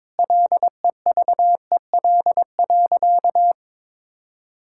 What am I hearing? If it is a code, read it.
Morse code, heard as a keyed tone, about 22 wpm, 700 Hz. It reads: LEVEL.
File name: Sentences with 7-Word Phrases - 22wpm